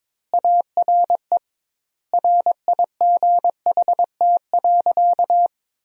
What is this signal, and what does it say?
Morse code, heard as a keyed tone, about 21 wpm, 700 Hz. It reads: ARE RIGHT.
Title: 2x Word-Spacing - Sentences from Top 500 Words - 22wpm